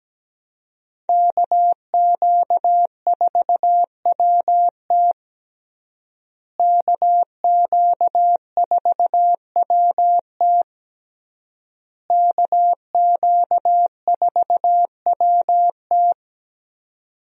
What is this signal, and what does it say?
Morse code, heard as a keyed tone, about 17 wpm, 700 Hz. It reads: KQ4WT KQ4WT KQ4WT
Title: USA Call Signs - 5 Characters - Repeated 3x - 17wpm